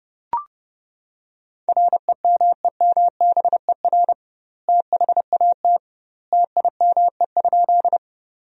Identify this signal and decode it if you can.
Morse code, heard as a keyed tone, about 30 wpm, 700 Hz. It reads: REMEMBER THAT TIME?
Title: 2x Word-Spacing - Sentences from Top 400 Words - 30wpm